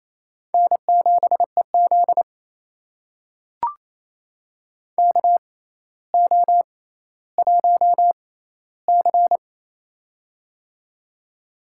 Morse code, heard as a keyed tone, about 28 wpm, spaced as for 14 wpm, 700 Hz. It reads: N7EZ KO1C
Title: USA Call Signs - 4 Characters - 6x Character Spacing - 28wpm